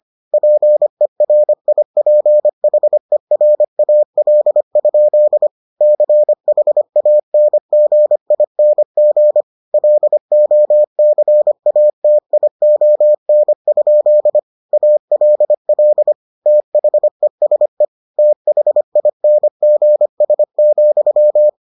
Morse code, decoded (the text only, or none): PERIPHERAL? CHANGING LOCATION? ALL THESE THINGS,